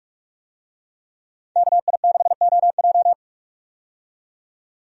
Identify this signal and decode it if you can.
Morse code, heard as a keyed tone, about 42 wpm, 700 Hz. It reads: KI6OJ
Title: USA Call Signs - 5 Characters - 45wpm